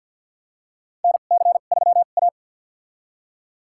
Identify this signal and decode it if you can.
Morse code, heard as a keyed tone, about 50 wpm, spaced as for 36 wpm, 700 Hz. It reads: NC3A
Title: USA Call Signs - 2x Character Spacing - 50wpm